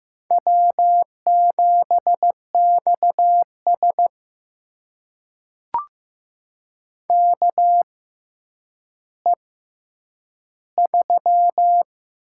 Morse code, heard as a keyed tone, about 15 wpm, spaced as for 8 wpm, 700 Hz. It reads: W7XS KE3
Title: USA Call Signs - 6x Character Spacing - 15wpm